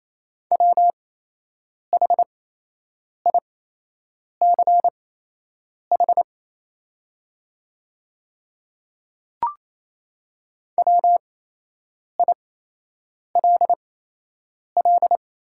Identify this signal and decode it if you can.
Morse code, heard as a keyed tone, about 28 wpm, spaced as for 8 wpm, 700 Hz. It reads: WHICH WILL
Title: 8x Character Spacing - Top 100 Words - 28wpm